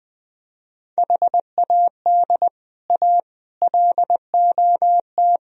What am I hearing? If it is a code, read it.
Morse code, heard as a keyed tone, about 20 wpm, 700 Hz. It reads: HAD A LOT